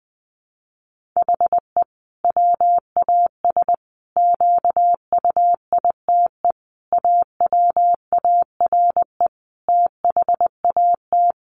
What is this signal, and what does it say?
Morse code, heard as a keyed tone, about 20 wpm, 700 Hz. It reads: HE WAS QUITE AWARE THAT
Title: Sentences from Top 1400 Words 20wpm